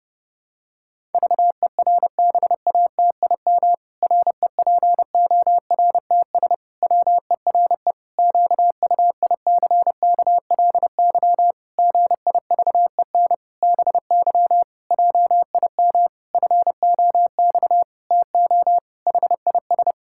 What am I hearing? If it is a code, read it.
Morse code, heard as a keyed tone, about 30 wpm, 700 Hz. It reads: VERBATIM REPORTS WERE QUICKLY GIVEN BY JIM FOX TO HIS
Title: Pangram Challenge v2 - 30wpm